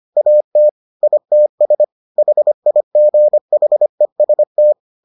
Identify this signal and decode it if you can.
Morse code, heard as a keyed tone, about 25 wpm, 600 Hz. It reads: AT ITS HIGHEST